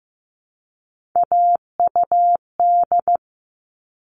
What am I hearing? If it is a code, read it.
Morse code, heard as a keyed tone, about 15 wpm, 700 Hz. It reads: AUD
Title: All Cut Numbers - 3-Digit 15wpm